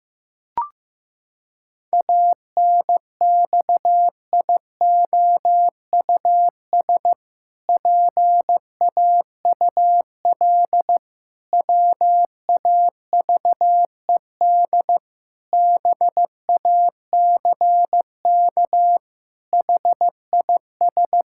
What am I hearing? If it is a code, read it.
Morse code, heard as a keyed tone, about 14 wpm, 700 Hz. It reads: ANXIOUS PAUL WAVED BACK HIS
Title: Pangram Challenge v2 - 15wpm